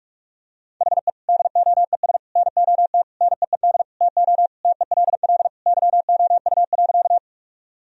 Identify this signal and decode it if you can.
Morse code, heard as a keyed tone, about 41 wpm, 700 Hz. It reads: HE DOES NOT NEED TO TELL YOU.
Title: Sentences from Top 1000 Words - Review Part 1 - 45wpm